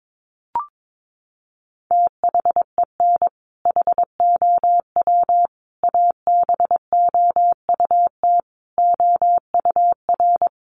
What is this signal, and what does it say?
Morse code, heard as a keyed tone, about 22 wpm, 700 Hz. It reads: THEN HOW ABOUT OUR
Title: Sentences from Top 900 Words 22wpm